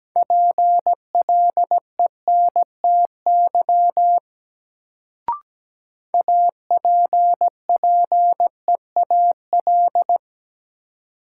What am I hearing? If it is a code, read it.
Morse code, heard as a keyed tone, about 17 wpm, 700 Hz. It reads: PLENTY APPEAL
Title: Sentences from Top 1300 Words - New Word Drill Alt - 17wpm